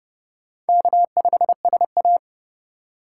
Morse code, heard as a keyed tone, about 28 wpm, 700 Hz. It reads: K5SA